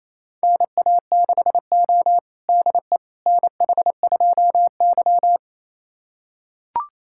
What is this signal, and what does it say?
Morse code, heard as a keyed tone, about 28 wpm, 700 Hz. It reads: NA6O DE NH2Y
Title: USA Callsign DE Callsign - 4 Characters - 28wpm